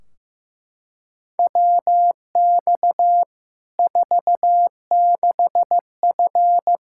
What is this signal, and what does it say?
Morse code, heard as a keyed tone, about 15 wpm, 700 Hz. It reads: WX 46F
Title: Practice QSO - Exchange 2 - Info - 15wpm